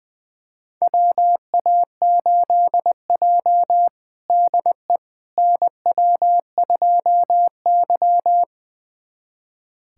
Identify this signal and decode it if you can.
Morse code, heard as a keyed tone, about 21 wpm, 700 Hz. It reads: WA8J DE NW2Y
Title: USA Callsign DE Callsign - 4 Characters - 20wpm